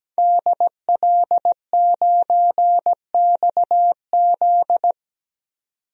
Morse code, heard as a keyed tone, about 17 wpm, 700 Hz. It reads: DL9XZ
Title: DXCC-100 Call Signs - 5 Characters - 17wpm